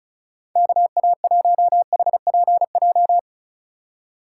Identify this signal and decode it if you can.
Morse code, heard as a keyed tone, about 36 wpm, 700 Hz. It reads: KA1HPJ
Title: USA Call Signs - 6 Characters - 4x Character Spacing - 35wpm